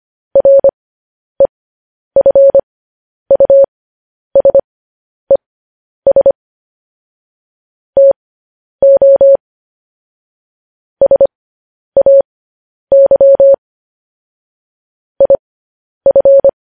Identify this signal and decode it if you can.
Morse code, heard as a keyed tone, about 25 wpm, spaced as for 10 wpm, 550 Hz. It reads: REFUSES TO SAY IF
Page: News Headlines in Morse Code at 10 WPM